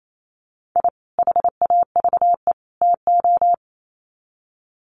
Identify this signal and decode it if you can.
Morse code, heard as a keyed tone, about 28 wpm, 700 Hz. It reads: I HAVE TO
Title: Sets of 3 Words 28wpm